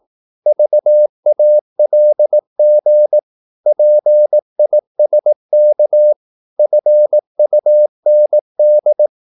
Morse code, heard as a keyed tone, about 18 wpm, 600 Hz. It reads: VALG PISK FUND
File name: Korte ord DK 18wpm